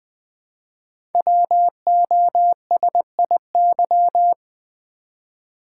Morse code, heard as a keyed tone, about 20 wpm, 700 Hz. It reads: WOSIY
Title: Code Groups - 5 Letter:Numbers - 20wpm